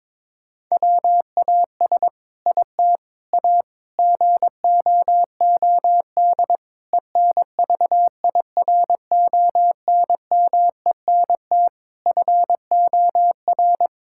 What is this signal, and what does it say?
Morse code, heard as a keyed tone, about 21 wpm, 700 Hz. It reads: WAS IT A GOOD ENVIRONMENT FOR
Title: Sentences from Top 1000 Words - Review Part 5 - 22wpm